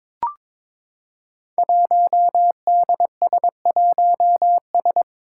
Morse code, heard as a keyed tone, about 21 wpm, 700 Hz. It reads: 1DS1S
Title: Five Challenging Letter-Number Combinations 22wpm